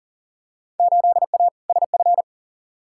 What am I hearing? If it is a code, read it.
Morse code, heard as a keyed tone, about 40 wpm, 700 Hz. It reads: 8A SF
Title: Field Day Exchanges Encore - 40wpm